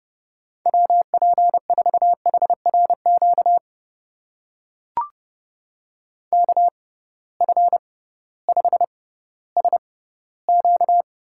Morse code, heard as a keyed tone, about 31 wpm, spaced as for 16 wpm, 700 Hz. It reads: WP4HRQ KF5SQ
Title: USA Call Signs - 6 Characters - 6x Character Spacing - 30wpm